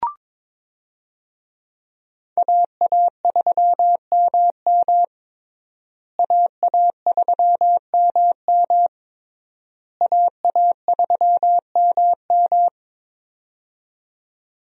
Morse code, heard as a keyed tone, about 21 wpm, 700 Hz. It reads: AA3MM AA3MM AA3MM
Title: USA Call Signs - 5 Characters - Repeated 3x - 22wpm